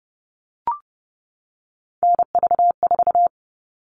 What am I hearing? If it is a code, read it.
Morse code, heard as a keyed tone, about 30 wpm, 700 Hz. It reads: NV4
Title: All Cut Numbers - 3-Digit 30wpm